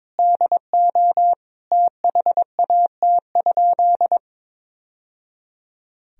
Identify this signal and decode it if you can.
Morse code, heard as a keyed tone, about 22 wpm, 700 Hz. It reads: DO THAT?
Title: Sentences from Top 1000 Words - Review Part 2 - 22wpm